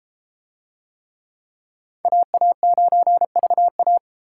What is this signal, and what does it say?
Morse code, heard as a keyed tone, about 31 wpm, 700 Hz. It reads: AA9VA